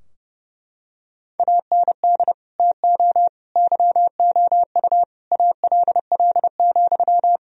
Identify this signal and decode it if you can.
Morse code, heard as a keyed tone, about 29 wpm, 700 Hz. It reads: AND TO YOU ALL,
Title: Sentences from Top 1000 Words - Review Part 3 - 30wpm